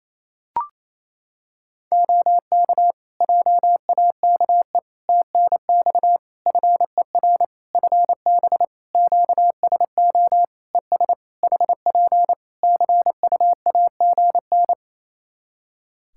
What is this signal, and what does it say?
Morse code, heard as a keyed tone, about 28 wpm, 700 Hz. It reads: OK JAKE TNX FER FB QSO ES HP CUAGN
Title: Practice QSO - Exchange 4 - Ending - 28wpm